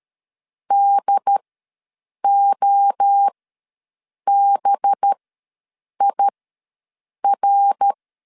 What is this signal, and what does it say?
Morse code, heard as a keyed tone, about 13 wpm, spaced as for 7 wpm, 800 Hz. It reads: DOBIR
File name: 7 GF9Z ws 12wpm